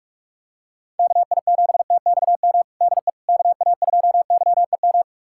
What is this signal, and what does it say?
Morse code, heard as a keyed tone, about 42 wpm, 700 Hz. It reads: KI7TXM DE KA2YEM
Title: USA Callsign DE Callsign - 6 Characters - 45wpm